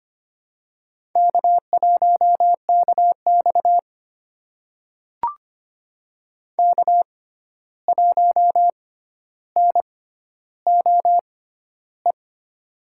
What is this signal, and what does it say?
Morse code, heard as a keyed tone, about 25 wpm, spaced as for 12 wpm, 700 Hz. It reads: K1KX K1NOE
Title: USA Call Signs - 6x Character Spacing - 25wpm